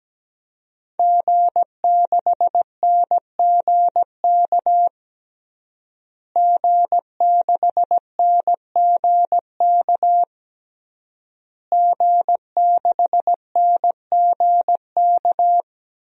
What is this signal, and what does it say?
Morse code, heard as a keyed tone, about 17 wpm, 700 Hz. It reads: G6NGK G6NGK G6NGK
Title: DXCC-100 Call Signs - 5 Characters - Repeated 3x - 17wpm